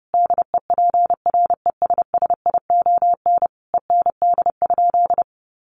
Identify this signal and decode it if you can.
Morse code, heard as a keyed tone, about 30 wpm, 700 Hz. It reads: DEPRESSION END?
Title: Sentences from Top 1300 Words 30wpm